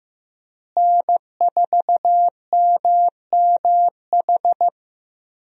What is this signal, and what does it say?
Morse code, heard as a keyed tone, about 16 wpm, 700 Hz. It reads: N4MMH
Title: USA Call Signs - 5 Characters - Repeated 3x - 15wpm